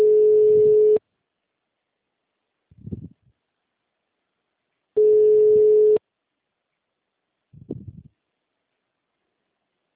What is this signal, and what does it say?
Ringback tone (Europe, 425 Hz).